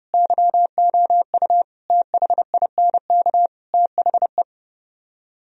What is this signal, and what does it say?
Morse code, heard as a keyed tone, about 30 wpm, 700 Hz. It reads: YOU THINK THE